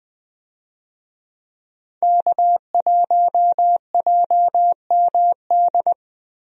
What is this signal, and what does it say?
Morse code, heard as a keyed tone, about 20 wpm, 700 Hz. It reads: K1JMD
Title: USA Call Signs - 5 Characters - Repeated 2x - 20wpm